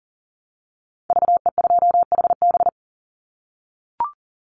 Morse code, heard as a keyed tone, about 38 wpm, 700 Hz. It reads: VE2HB
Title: Call Signs - 40wpm